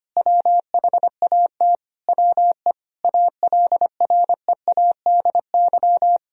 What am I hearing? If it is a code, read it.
Morse code, heard as a keyed tone, about 25 wpm, 700 Hz. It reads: WHAT WE ALREADY
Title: Sentences from Top 1000 Words - Review Part 1 - 25wpm